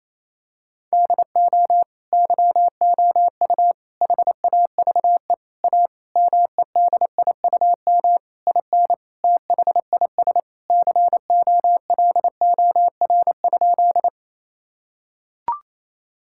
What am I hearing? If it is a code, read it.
Morse code, heard as a keyed tone, about 28 wpm, 700 Hz. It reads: DO YOU HAVE A MEDIUM IN THIS COLOR?
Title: Sentences from Top 1000 Words - Review Part 6 - 28wpm